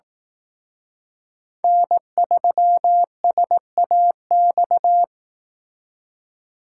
Morse code, heard as a keyed tone, about 18 wpm, 700 Hz. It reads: N3SAX
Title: Speed Racing - USA 1x3 Call Signs 15wpm